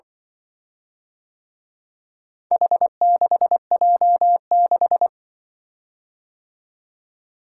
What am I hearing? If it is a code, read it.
Morse code, heard as a keyed tone, about 23 wpm, 700 Hz. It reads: H6J6
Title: Speed-Racing - Four Letter-Number Combinations - 20wpm